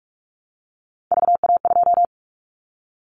Morse code, heard as a keyed tone, about 46 wpm, 700 Hz. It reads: VA2
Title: Callsign Prefixes for Top 50 DXCC - 45wpm